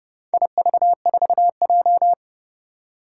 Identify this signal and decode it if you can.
Morse code, heard as a keyed tone, about 28 wpm, 700 Hz. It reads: IV4J
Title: Four Challenging Letter-Number Combinations 30wpm